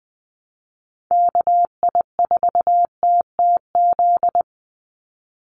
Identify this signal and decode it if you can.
Morse code, heard as a keyed tone, about 20 wpm, 700 Hz. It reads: KI4TTZ
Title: USA Call Signs 20wpm